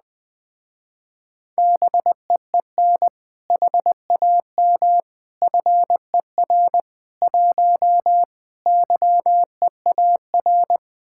Morse code, heard as a keyed tone, about 20 wpm, 700 Hz. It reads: BEEN HAM FER 1 YEAR
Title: Speed-Racing Common QSO Phrases Encore - 17wpm